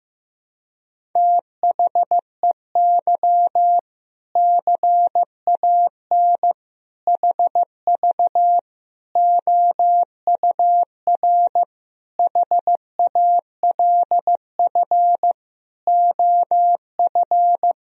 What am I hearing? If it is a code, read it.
Morse code, heard as a keyed tone, about 15 wpm, 700 Hz. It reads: THEY CAN HV OUR HALF OF
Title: Sentences from Top 400 Words and Abbreviations 15wpm